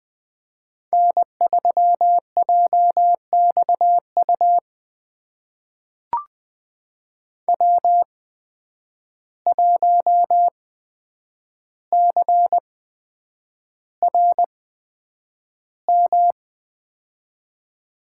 Morse code, heard as a keyed tone, about 20 wpm, spaced as for 9 wpm, 700 Hz. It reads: N3JXU W1CRM
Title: USA Call Signs - 5 Characters - 8x Character Spacing - 20wpm